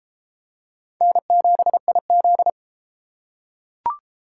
Morse code, heard as a keyed tone, about 30 wpm, 700 Hz. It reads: N7IZ